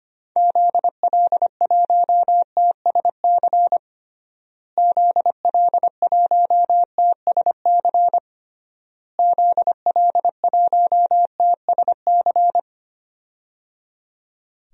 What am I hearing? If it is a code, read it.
Morse code, heard as a keyed tone, about 25 wpm, 700 Hz. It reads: ZL1TSC ZL1TSC ZL1TSC
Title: DXCC-100 Call Signs - 6 Characters - Repeated 3x - 25wpm